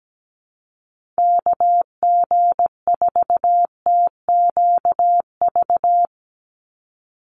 Morse code, heard as a keyed tone, about 17 wpm, 700 Hz. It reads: KG4TQV
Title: USA Call Signs 17wpm